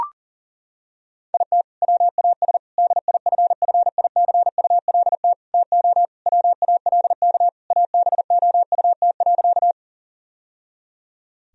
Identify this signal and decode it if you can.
Morse code, heard as a keyed tone, about 40 wpm, 700 Hz. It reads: IT WAS DIFFICULT TO WALK ABOUT.
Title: Sentences from Top 1000 Words - Review Part 1 - 40wpm